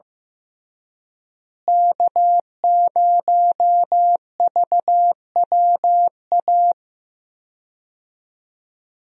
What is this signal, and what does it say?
Morse code, heard as a keyed tone, about 15 wpm, 700 Hz. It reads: K0VWA
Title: USA Call Signs - 5 Characters - Repeated 3x - 15wpm